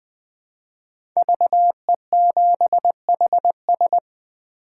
Morse code, heard as a keyed tone, about 20 wpm, 700 Hz. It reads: VE7HS
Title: K1USN SST QSO Practice - 20wpm